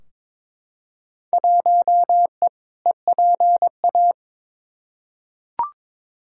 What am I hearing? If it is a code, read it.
Morse code, heard as a keyed tone, about 22 wpm, 700 Hz. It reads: 1E EPA
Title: Field Day Exchanges - 22wpm